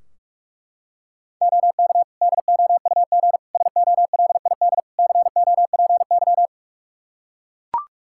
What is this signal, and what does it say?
Morse code, heard as a keyed tone, about 41 wpm, 700 Hz. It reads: OK DOUG SOLID COPY
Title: K1USN SST QSO Practice 45wpm